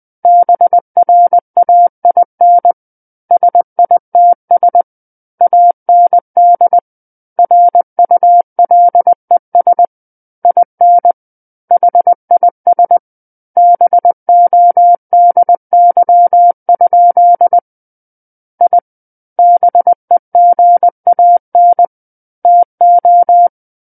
Morse code, heard as a keyed tone, about 20 wpm, 700 Hz. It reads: BRAIN SITS AND RULES IN HIS BODY? I BEGAN TO
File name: 20 WPM